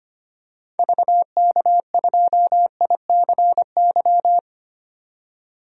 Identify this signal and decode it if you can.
Morse code, heard as a keyed tone, about 25 wpm, 700 Hz. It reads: VK2ICY
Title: DXCC-100 Call Signs - 6 Characters - 25wpm